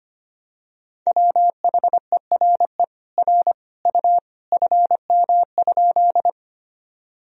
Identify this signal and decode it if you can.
Morse code, heard as a keyed tone, about 25 wpm, 700 Hz. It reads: WHERE R U FM?
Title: Sentences from Top 200 Words and Abbreviations 25wpm